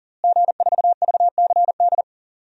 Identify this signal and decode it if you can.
Morse code, heard as a keyed tone, about 40 wpm, 700 Hz. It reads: G4VCD
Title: DXCC-100 Call Signs - 5 Characters - Repeated 2x - 40wpm